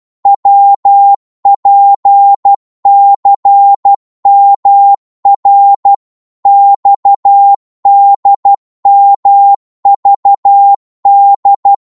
Morse code, heard as a keyed tone, about 12 wpm, 800 Hz. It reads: WPCMR XDMVD
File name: Morse / ltr-12wpm